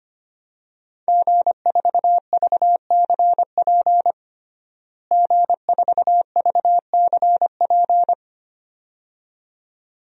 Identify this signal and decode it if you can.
Morse code, heard as a keyed tone, about 25 wpm, 700 Hz. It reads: G4VCP G4VCP
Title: DXCC-100 Call Signs - 5 Characters - Repeated 2x - 25wpm